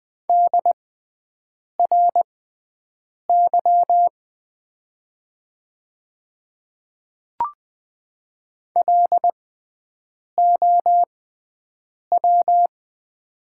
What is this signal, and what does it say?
Morse code, heard as a keyed tone, about 20 wpm, spaced as for 7 wpm, 700 Hz. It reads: DRY LOW
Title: 6x Character Spacing - Top 500 Words - 20wpm